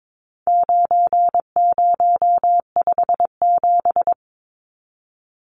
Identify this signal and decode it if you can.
Morse code, heard as a keyed tone, about 22 wpm, 700 Hz. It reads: 9057
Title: Numbers - 4-Digit 22wpm